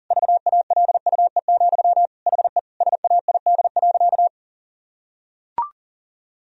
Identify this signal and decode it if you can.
Morse code, heard as a keyed tone, about 40 wpm, 700 Hz. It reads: VALUE, HE SAID.
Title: Sentences from Top 1000 Words - Review Part 3 - 40wpm